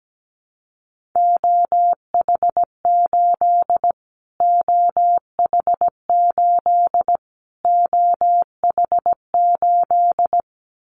Morse code, heard as a keyed tone, about 17 wpm, 700 Hz. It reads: OH8 OH8 OH8
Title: Callsign Prefixes for Top 25 DXCC - Repeated 3x - 17wpm